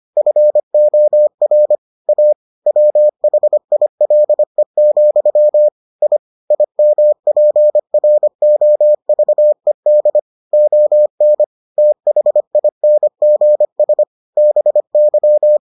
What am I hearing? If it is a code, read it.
Morse code, heard as a keyed tone, about 25 wpm, 600 Hz. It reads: FOR A WHILE, I IMPROVED ON THINGS BY